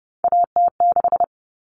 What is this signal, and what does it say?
Morse code, heard as a keyed tone, about 30 wpm, 700 Hz. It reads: AT6
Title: Common Cut Numbers - 3-Digit 30wpm